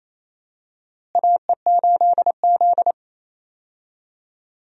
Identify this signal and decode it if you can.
Morse code, heard as a keyed tone, about 28 wpm, 700 Hz. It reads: AE8Z
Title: USA Call Signs - 4 Characters - Repeated 2x - 28wpm